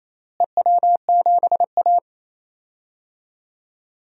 Morse code, heard as a keyed tone, about 28 wpm, 700 Hz. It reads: EW7A
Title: DXCC-100 Call Signs - 4 Characters - Repeated 2x - 28wpm